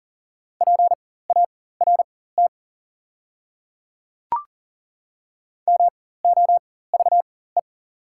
Morse code, heard as a keyed tone, about 40 wpm, spaced as for 19 wpm, 700 Hz. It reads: PART MOVE
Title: 4x Character Spacing - Top 200 Words - 40wpm